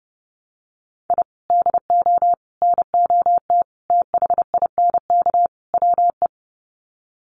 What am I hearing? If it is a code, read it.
Morse code, heard as a keyed tone, about 30 wpm, 700 Hz. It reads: I DO NOT THINK WE
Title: Sets of 5 Words 30wpm